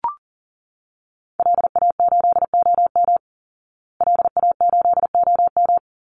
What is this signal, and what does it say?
Morse code, heard as a keyed tone, about 37 wpm, 700 Hz. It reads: LA8OM LA8OM
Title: Call Signs Repeated 3x - 40wpm